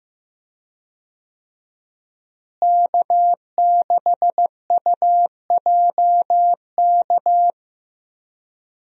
Morse code, heard as a keyed tone, about 15 wpm, 700 Hz. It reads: K6UJK
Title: USA Call Signs - 5 Characters - Repeated 2x - 15wpm